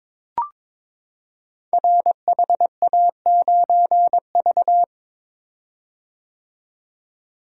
Morse code, heard as a keyed tone, about 22 wpm, 700 Hz. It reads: RHA9V
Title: Code Groups - 5 Letter:Numbers - 22wpm